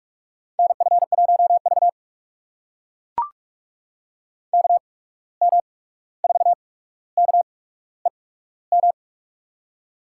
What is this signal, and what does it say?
Morse code, heard as a keyed tone, about 43 wpm, spaced as for 17 wpm, 700 Hz. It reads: NF1V KM4KEM
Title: USA Call Signs - 8x Character Spacing - 45wpm